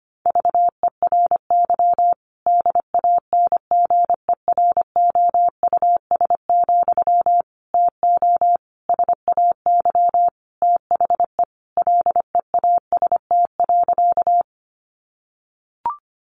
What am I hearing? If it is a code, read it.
Morse code, heard as a keyed tone, about 25 wpm, 700 Hz. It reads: VERY DANGEROUS, TO SAY THE LEAST.
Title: Sentences from Top 1200 Words 25wpm